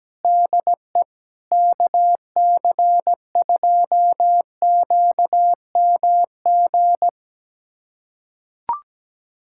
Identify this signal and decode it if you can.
Morse code, heard as a keyed tone, about 17 wpm, 700 Hz. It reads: DE KC2QMG
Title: USA Callsign DE Callsign - 6 Characters - 17wpm